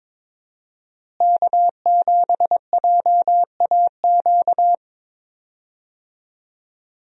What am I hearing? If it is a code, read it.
Morse code, heard as a keyed tone, about 22 wpm, 700 Hz. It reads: K7JAQ
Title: USA Call Signs - 5 Characters - Repeated 2x - 22wpm